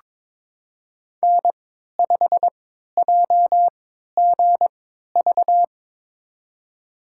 Morse code, heard as a keyed tone, about 22 wpm, spaced as for 12 wpm, 700 Hz. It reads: N5JGV
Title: USA Call Signs - 3x Character Spacing - 22wpm